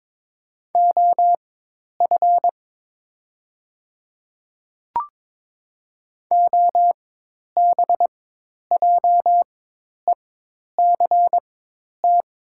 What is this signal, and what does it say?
Morse code, heard as a keyed tone, about 23 wpm, spaced as for 10 wpm, 700 Hz. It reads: OF OBJECT
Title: 4x Character Spacing - Top 500 Words - 22wpm